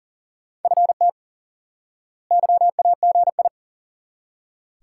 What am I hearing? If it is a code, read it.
Morse code, heard as a keyed tone, about 40 wpm, 700 Hz. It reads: FT YAGI
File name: course / instant-qso-element-course-3 / IQE Course 3 - Lesson 89 - YAGI - Two Elements - 40wpm